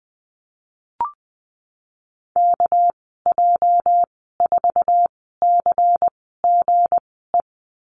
Morse code, heard as a keyed tone, about 20 wpm, 700 Hz. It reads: K J 4 C G E
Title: 231 Lesson 37g - Letter x - Call Signs 20wpm